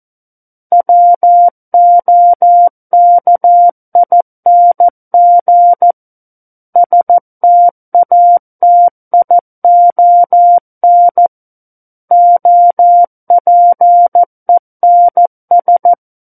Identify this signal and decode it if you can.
Morse code, heard as a keyed tone, about 14 wpm, 700 Hz. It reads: WOKING STATION OPENS
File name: Book One / 05-Chapter 5 - 14 WPM